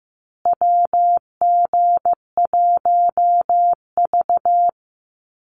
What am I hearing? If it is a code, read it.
Morse code, heard as a keyed tone, about 15 wpm, 700 Hz. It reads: WG1V
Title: USA Call Signs Repeated 3x 15wpm